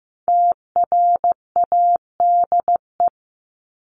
Morse code, heard as a keyed tone, about 15 wpm, 700 Hz. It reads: TRADE
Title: Top 501-750 Words 15wpm